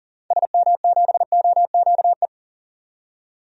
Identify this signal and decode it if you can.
Morse code, heard as a keyed tone, about 40 wpm, 700 Hz. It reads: SM7OQE
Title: DXCC-100 Call Signs - 6 Characters - 40wpm